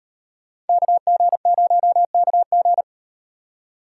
Morse code, heard as a keyed tone, about 38 wpm, 700 Hz. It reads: KG0KG